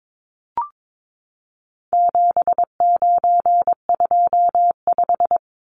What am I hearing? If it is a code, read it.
Morse code, heard as a keyed tone, about 22 wpm, 700 Hz. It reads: 7925
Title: Numbers - 4-Digit 22wpm